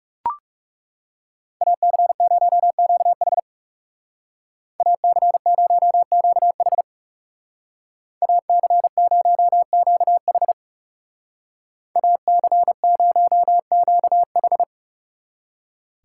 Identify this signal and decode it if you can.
Morse code, heard as a keyed tone, about 39 wpm, 700 Hz. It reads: AC0QH AC0QH AC0QH AC0QH